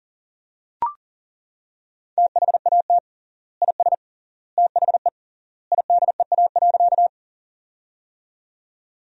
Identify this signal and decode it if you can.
Morse code, heard as a keyed tone, about 40 wpm, 700 Hz. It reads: THAT IS THE IDEA.
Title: 3x Word-Spacing - Sentences from Top 300 Words - 40wpm